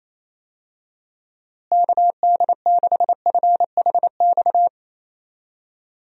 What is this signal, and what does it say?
Morse code, heard as a keyed tone, about 27 wpm, 700 Hz. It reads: KD6FHX
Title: USA Call Signs - 6 Characters - Repeated 2x - 28wpm